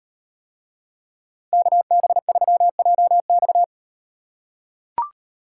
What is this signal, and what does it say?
Morse code, heard as a keyed tone, about 38 wpm, 700 Hz. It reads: KB3JX